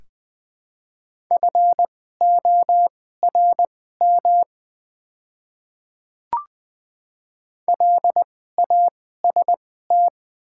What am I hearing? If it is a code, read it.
Morse code, heard as a keyed tone, about 20 wpm, spaced as for 14 wpm, 700 Hz. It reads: FORM LAST
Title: 2x Character Spacing - Top 300 Words - 20wpm